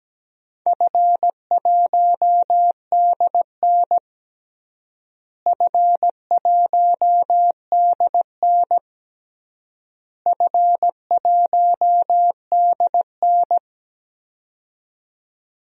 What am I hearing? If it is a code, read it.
Morse code, heard as a keyed tone, about 17 wpm, 700 Hz. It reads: F1DN F1DN F1DN
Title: DXCC-100 Call Signs - 4 Characters - Repeated 3x - 17wpm